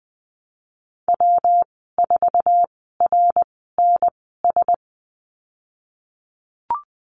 Morse code, heard as a keyed tone, about 20 wpm, spaced as for 15 wpm, 700 Hz. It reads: W4RNS
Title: 161 Lesson 26g - Letter g - Call Signs 20wpm